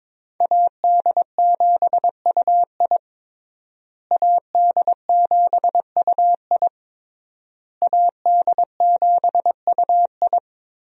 Morse code, heard as a keyed tone, about 22 wpm, 700 Hz. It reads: AD7UI AD7UI AD7UI
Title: USA Call Signs - 5 Characters - Repeated 3x - 22wpm